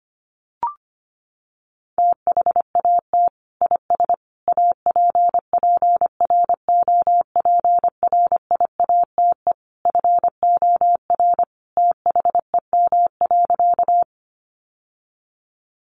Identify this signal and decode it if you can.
Morse code, heard as a keyed tone, about 25 wpm, 700 Hz. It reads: THAT IS APPROPRIATE FOR THEM.
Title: Sentences from Top 900 Words 25wpm